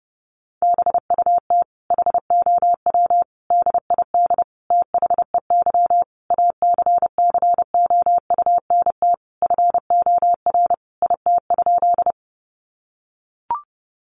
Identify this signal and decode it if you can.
Morse code, heard as a keyed tone, about 30 wpm, 700 Hz. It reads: BUT HOW DID THEY ACCOUNT FOR IT?
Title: Sentences from Top 900 Words 30wpm